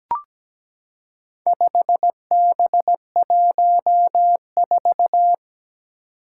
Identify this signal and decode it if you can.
Morse code, heard as a keyed tone, about 17 wpm, 700 Hz. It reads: 5B14
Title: Four Challenging Letter-Number Combinations 17wpm